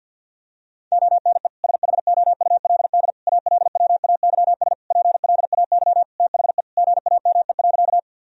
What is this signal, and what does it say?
Morse code, heard as a keyed tone, about 49 wpm, 700 Hz. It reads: ONE SHOULD ALWAYS PLAY THE GAME.
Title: Sentences from Top 1000 Words - Review Part 3 - 50wpm